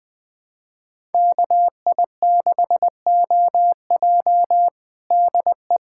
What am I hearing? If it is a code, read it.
Morse code, heard as a keyed tone, about 20 wpm, 700 Hz. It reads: KI6OJ DE
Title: Practice QSO - Exchange 4 - Ending - 20wpm